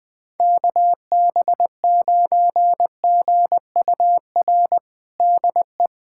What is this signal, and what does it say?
Morse code, heard as a keyed tone, about 20 wpm, 700 Hz. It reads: KB9GUR DE
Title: USA Callsign DE Callsign 20wpm